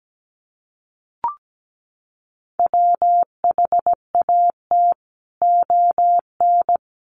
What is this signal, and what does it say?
Morse code, heard as a keyed tone, about 17 wpm, 700 Hz. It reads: WHAT ON
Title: Sentences from Top 1100 Words 17wpm